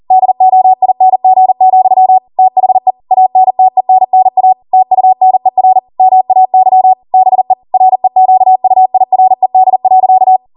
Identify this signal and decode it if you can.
Morse code, heard as a keyed tone, about 40 wpm, 750 Hz. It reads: DOING, THE ANTENNA TUNER MAY BE REQUIRED.